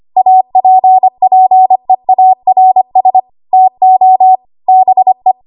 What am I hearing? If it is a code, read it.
Morse code, heard as a keyed tone, about 25 wpm, 750 Hz. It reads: APPEARS TO BE
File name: Archive / 25 WPM / 25WPM